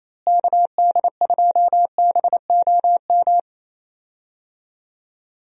Morse code, heard as a keyed tone, about 28 wpm, 700 Hz. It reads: KD2BOM
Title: USA Call Signs - 6 Characters - Repeated 3x - 28wpm